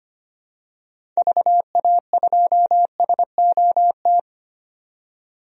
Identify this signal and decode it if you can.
Morse code, heard as a keyed tone, about 25 wpm, 700 Hz. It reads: VA2SOT
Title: DXCC-100 Call Signs - 6 Characters - 25wpm